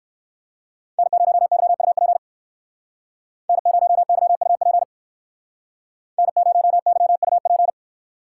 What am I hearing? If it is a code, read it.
Morse code, heard as a keyed tone, about 58 wpm, 700 Hz. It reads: N0YUP N0YUP N0YUP